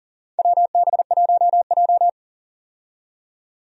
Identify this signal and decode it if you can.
Morse code, heard as a keyed tone, about 40 wpm, 700 Hz. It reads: WB1J
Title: USA Call Signs - 4 Characters - 40wpm